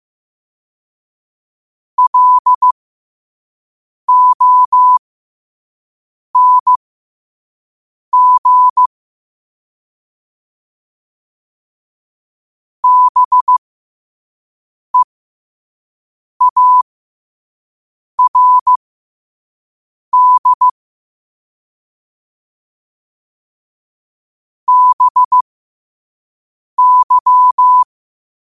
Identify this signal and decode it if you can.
Morse code, heard as a keyed tone, about 15 wpm, spaced as for 5 wpm, 1000 Hz. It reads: LONG BEARD BY
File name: qotdfiles / 5wpm